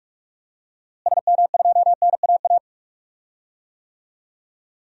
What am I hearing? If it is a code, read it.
Morse code, heard as a keyed tone, about 43 wpm, 700 Hz. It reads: SM2NAA
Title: DXCC-100 Call Signs - 6 Characters - Repeated 3x - 45wpm